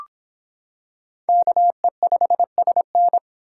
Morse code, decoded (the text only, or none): KE5SN